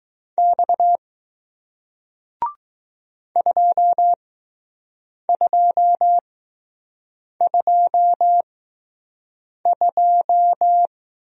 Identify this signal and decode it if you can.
Morse code, heard as a keyed tone, about 20 wpm, spaced as for 6 wpm, 700 Hz. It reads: X 2222